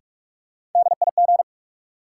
Morse code, heard as a keyed tone, about 46 wpm, 700 Hz. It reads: DIG
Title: Sentences from Top 1600 Words - New Word Drill - 45wpm